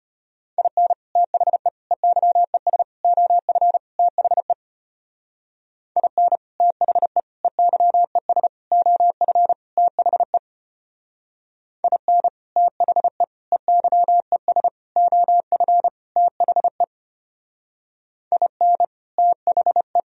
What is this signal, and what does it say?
Morse code, heard as a keyed tone, about 32 wpm, 700 Hz. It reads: IN THE EYES OF THE IN THE EYES OF THE IN THE EYES OF THE IN THE